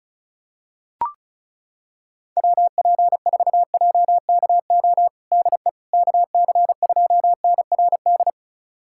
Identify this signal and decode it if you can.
Morse code, heard as a keyed tone, about 34 wpm, 700 Hz. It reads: WP4JKO DE KC2NRD
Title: USA Callsign DE Callsign 35wpm